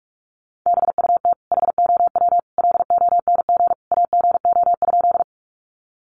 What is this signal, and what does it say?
Morse code, heard as a keyed tone, about 44 wpm, 700 Hz. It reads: BUT HOW LONG AGO?
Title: Sentences from Top 500 Words 45wpm